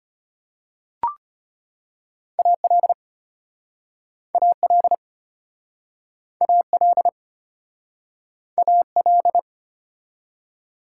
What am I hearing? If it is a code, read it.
Morse code, heard as a keyed tone, about 33 wpm, 700 Hz. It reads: AL AL AL AL